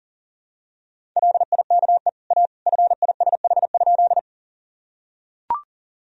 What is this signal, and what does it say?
Morse code, heard as a keyed tone, about 37 wpm, 700 Hz. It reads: LIKE A FISH?
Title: Sentences from Top 1000 Words - Review Part 1 - 40wpm